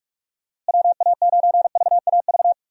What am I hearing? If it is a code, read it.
Morse code, heard as a keyed tone, about 45 wpm, 700 Hz. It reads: WA9VAV
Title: USA Call Signs - 6 Characters - Repeated 3x - 45wpm